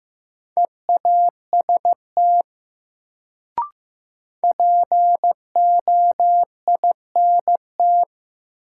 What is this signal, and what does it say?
Morse code, heard as a keyed tone, about 15 wpm, 700 Hz. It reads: EAST POINT
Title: Sets of 3 Words - New Word Drill - 15wpm